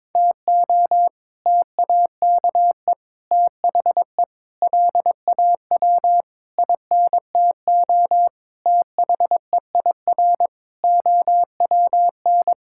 Morse code, heard as a keyed tone, about 22 wpm, 700 Hz. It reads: TO TAKE THE LAW INTO THEIR OWN
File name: practice / 22wpm / Sets of 8 Words - 22wpm